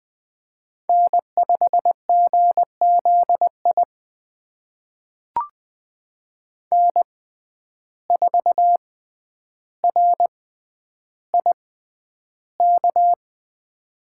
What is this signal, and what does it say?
Morse code, heard as a keyed tone, about 21 wpm, spaced as for 10 wpm, 700 Hz. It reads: N5GZI N4RIK
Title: USA Call Signs - 5 Characters - 6x Character Spacing - 20wpm